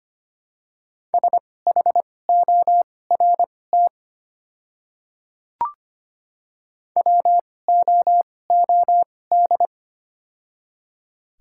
Morse code, heard as a keyed tone, about 25 wpm, spaced as for 18 wpm, 700 Hz. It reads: SHORT WOOD
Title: 2x Character Spacing - Top 500 Words - 25wpm